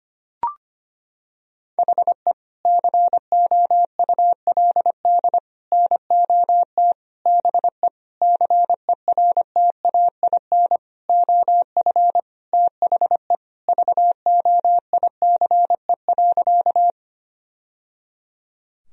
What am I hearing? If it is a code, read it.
Morse code, heard as a keyed tone, about 25 wpm, 700 Hz. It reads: HE COULD NOT BE CERTAIN OF THE VOICE.
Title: Sentences from Top 1000 Words - Review Part 2 - 25wpm